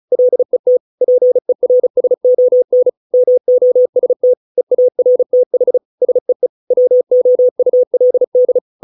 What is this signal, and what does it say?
Morse code, heard as a keyed tone, about 34 wpm, 500 Hz. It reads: LET PERSON MOST EARTH SEE WOULD